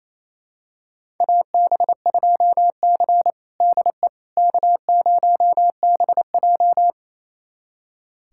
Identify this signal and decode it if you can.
Morse code, heard as a keyed tone, about 28 wpm, 700 Hz. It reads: AB2C DE K0BJ
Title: USA Callsign DE Callsign - 4 Characters - 28wpm